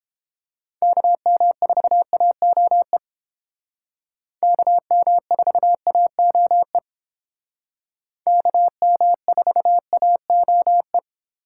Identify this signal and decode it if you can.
Morse code, heard as a keyed tone, about 29 wpm, 700 Hz. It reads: KM4AOE KM4AOE KM4AOE